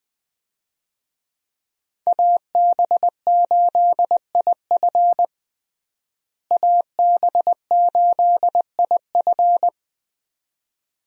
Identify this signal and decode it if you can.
Morse code, heard as a keyed tone, about 20 wpm, 700 Hz. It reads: AB8IF AB8IF
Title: USA Call Signs - 5 Characters - Repeated 2x - 20wpm